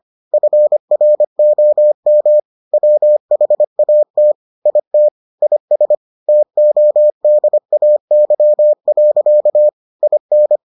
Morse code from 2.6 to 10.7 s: WHAT IT IS TODAY. IN

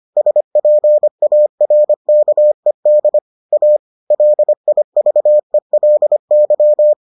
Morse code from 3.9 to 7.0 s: LIVELY

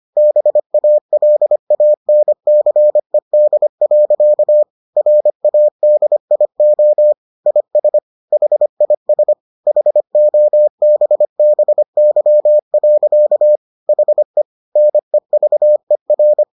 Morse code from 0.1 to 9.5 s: BALANCED. RADIO IS HIS

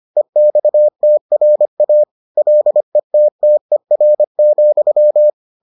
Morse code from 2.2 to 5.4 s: LETTER,

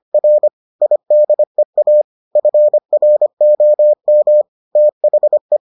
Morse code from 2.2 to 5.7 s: FROM THE

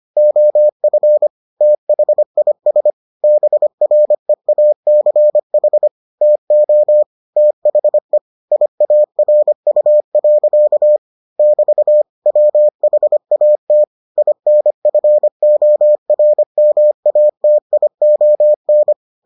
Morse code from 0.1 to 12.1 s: OF THIS BREACH TO THE IARU. <BT>

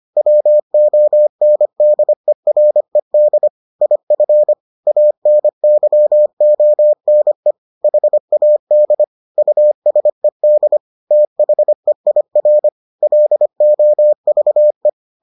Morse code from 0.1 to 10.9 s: WONDERED IF ANYONE HAD USED

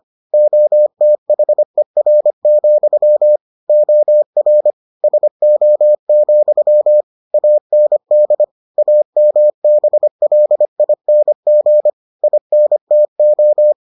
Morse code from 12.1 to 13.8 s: INTO

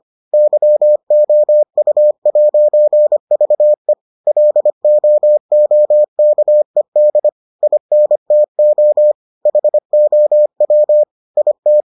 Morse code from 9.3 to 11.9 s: HOW IT

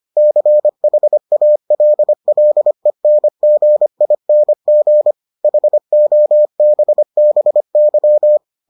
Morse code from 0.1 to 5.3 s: CHALLENGING

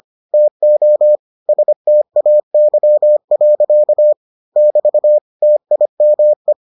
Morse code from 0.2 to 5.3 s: TO STAY. <BT>

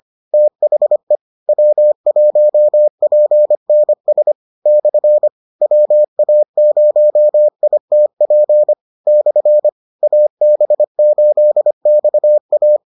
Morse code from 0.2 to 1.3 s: THE

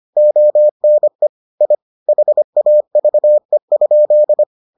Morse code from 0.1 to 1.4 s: ONE

